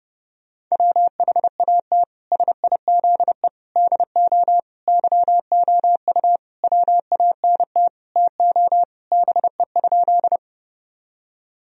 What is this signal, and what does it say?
Morse code, heard as a keyed tone, about 32 wpm, 700 Hz. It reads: WHAT SIZE DO YOU WANT TO BE?
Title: Sentences from Top 1000 Words - Review Part 2 - 30wpm